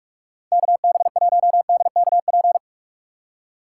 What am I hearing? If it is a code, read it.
Morse code, heard as a keyed tone, about 45 wpm, 690 Hz. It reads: KB1DKP
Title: USA Call Signs - 6x Character Spacing - 45wpm